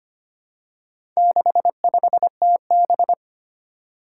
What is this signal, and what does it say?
Morse code, heard as a keyed tone, about 25 wpm, 700 Hz. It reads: 65TB
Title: Four Challenging Letter-Number Combinations 25wpm